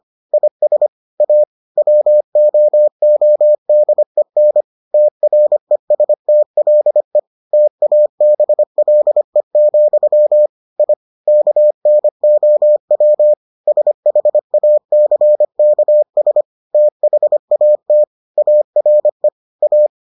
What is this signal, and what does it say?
Morse code, heard as a keyed tone, about 25 wpm, 600 Hz. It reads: IS A WOODEN TRESTLE TABLE, I KNOW SHACKS THAT ARE A